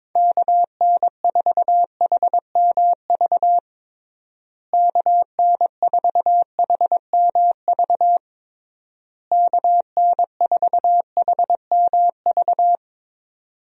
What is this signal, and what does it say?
Morse code, heard as a keyed tone, about 22 wpm, 700 Hz. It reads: KN4HMV KN4HMV KN4HMV
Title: USA Call Signs - 6 Characters - Repeated 3x - 22wpm